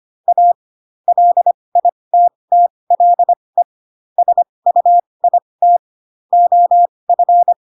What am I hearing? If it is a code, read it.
Morse code, heard as a keyed tone, about 25 wpm, spaced as for 20 wpm, 700 Hz. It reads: A LITTLE SUIT OF
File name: Pinocchio / Pinocchio ch8 15 25 ik1hge